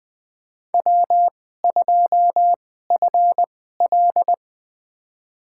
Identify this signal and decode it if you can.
Morse code, heard as a keyed tone, about 20 wpm, spaced as for 14 wpm, 700 Hz. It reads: W2FL
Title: USA Call Signs - 4 Characters - 2x Character Spacing - 20wpm